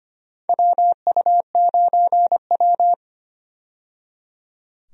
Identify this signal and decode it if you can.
Morse code, heard as a keyed tone, about 25 wpm, 700 Hz. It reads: WU9W
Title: USA Call Signs - 4 Characters - 25wpm